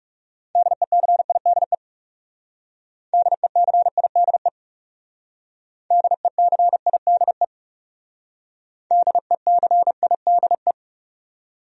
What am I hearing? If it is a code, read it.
Morse code, heard as a keyed tone, about 37 wpm, 700 Hz. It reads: DECIDE DECIDE DECIDE DECIDE
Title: Speed-Racing Top 500 Words 30wpm